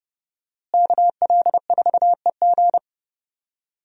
Morse code, heard as a keyed tone, about 32 wpm, 700 Hz. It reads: KL4EG
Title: USA Call Signs - 5 Characters - Repeated 2x - 30wpm